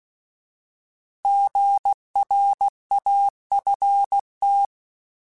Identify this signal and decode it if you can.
Morse code, heard as a keyed tone, about 16 wpm, 800 Hz. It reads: GRAFT
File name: Alphabetical / Alphabetical mp3s / Gs @15